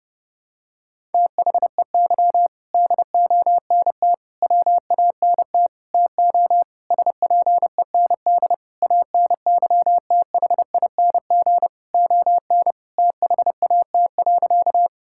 Morse code, heard as a keyed tone, about 30 wpm, 700 Hz. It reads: THEY DONT WANT TO SPEND ANYTHING ON THAT.
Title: Sentences from Top 1000 Words - Review Part 5 - 30wpm